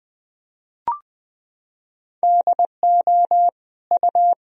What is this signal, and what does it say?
Morse code, heard as a keyed tone, about 21 wpm, 700 Hz. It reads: DO U
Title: Sentences from Top 200 Words and Abbreviations 20wpm